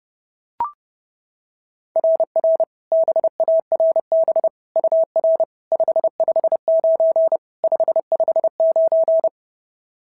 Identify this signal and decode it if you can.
Morse code, heard as a keyed tone, about 30 wpm, 690 Hz. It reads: RR BARB UR 559 559
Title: SOTA QSO Practice 30wpm